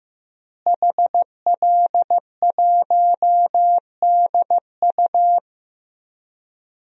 Morse code, heard as a keyed tone, about 15 wpm, 700 Hz. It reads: HL1DU
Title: DXCC-100 Call Signs - 5 Characters - Repeated 2x - 15wpm